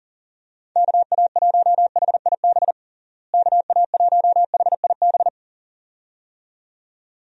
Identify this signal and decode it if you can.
Morse code, heard as a keyed tone, about 40 wpm, 700 Hz. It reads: KA1HIB KA1HIB
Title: USA Call Signs - 6 Characters - Repeated 2x - 40wpm